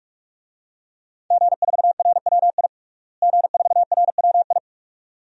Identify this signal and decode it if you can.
Morse code, heard as a keyed tone, about 46 wpm, 700 Hz. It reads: G4RWI G4RWI
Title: DXCC-100 Call Signs - 5 Characters - Repeated 2x - 45wpm